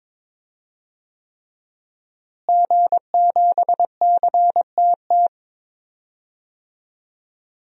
Morse code, heard as a keyed tone, about 22 wpm, 700 Hz. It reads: G7CTT
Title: DXCC-100 Call Signs - 5 Characters - Repeated 3x - 22wpm